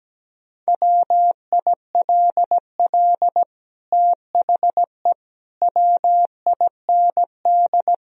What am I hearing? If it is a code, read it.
Morse code, heard as a keyed tone, about 17 wpm, 700 Hz. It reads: WILL THE WIND
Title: Sentences from Top 1000 Words - Review Part 2 - 17wpm